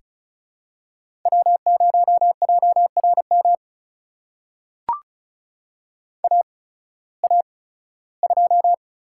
Morse code, heard as a keyed tone, about 36 wpm, spaced as for 19 wpm, 690 Hz. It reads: W0JRM AA2
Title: USA Call Signs - 5 Characters - 8x Character Spacing - 35wpm